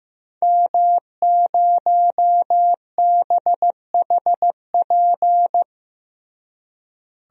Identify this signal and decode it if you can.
Morse code, heard as a keyed tone, about 16 wpm, 700 Hz. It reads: M0BHP
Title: DXCC-100 Call Signs - 5 Characters - Repeated 3x - 15wpm